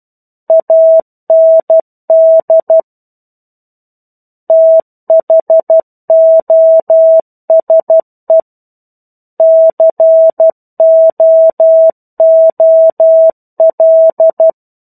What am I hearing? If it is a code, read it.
Morse code, heard as a keyed tone, about 12 wpm, 650 Hz. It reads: AND THOSE COOL
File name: 01-Chapter-01 12 WPM